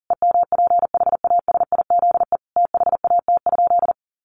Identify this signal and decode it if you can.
Morse code, heard as a keyed tone, about 40 wpm, 700 Hz. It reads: EMPHASIZE THAT?
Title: Sentences from Top 1500 Words 40wpm